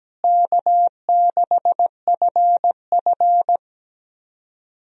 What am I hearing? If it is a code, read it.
Morse code, heard as a keyed tone, about 17 wpm, 700 Hz. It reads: K6FF
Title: USA Call Signs - 4 Characters - Repeated 2x - 17wpm